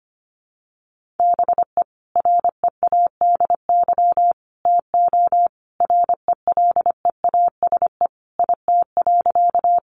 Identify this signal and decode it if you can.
Morse code, heard as a keyed tone, about 25 wpm, 700 Hz. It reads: BE READY TO RELEASE IT.
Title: Sentences from Top 1100 Words 25wpm